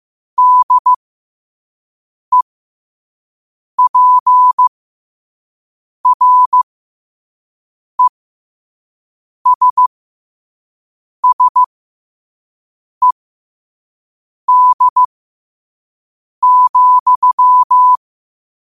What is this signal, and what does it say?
Morse code, heard as a keyed tone, about 15 wpm, spaced as for 5 wpm, 1000 Hz. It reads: DEPRESSED,
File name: qotdfiles / 5wpm